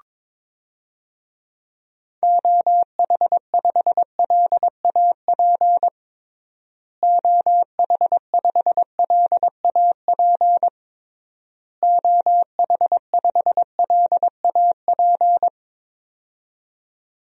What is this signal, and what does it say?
Morse code, heard as a keyed tone, about 22 wpm, 700 Hz. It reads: OH5LAP OH5LAP OH5LAP
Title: DXCC-100 Call Signs - 6 Characters - Repeated 3x - 22wpm